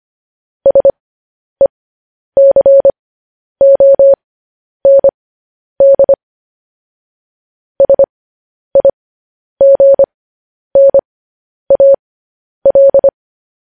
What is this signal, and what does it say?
Morse code, heard as a keyed tone, about 25 wpm, spaced as for 10 wpm, 550 Hz. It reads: SECOND SIGNAL